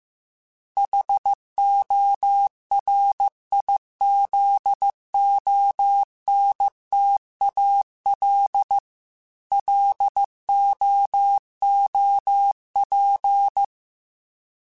Morse code, heard as a keyed tone, about 15 wpm, 800 Hz. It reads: HORIZONTAL LOOP
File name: fromSMs / FromSMs mp3s / Antennas@13